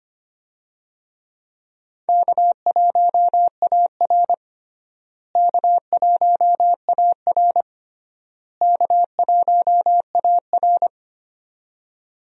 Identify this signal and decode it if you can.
Morse code, heard as a keyed tone, about 25 wpm, 700 Hz. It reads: K1AR K1AR K1AR
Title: USA Call Signs - 4 Characters - Repeated 3x - 25wpm